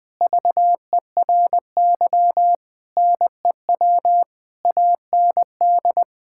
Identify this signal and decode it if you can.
Morse code, heard as a keyed tone, about 20 wpm, 700 Hz. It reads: VERY NEW AND
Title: Sentences from Top 1000 Words - Review Part 6 - 20wpm